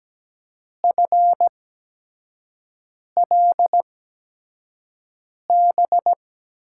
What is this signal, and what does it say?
Morse code, heard as a keyed tone, about 17 wpm, spaced as for 5 wpm, 700 Hz. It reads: FLB
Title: USA Call Signs - 6 Characters - 8x Character Spacing - 17wpm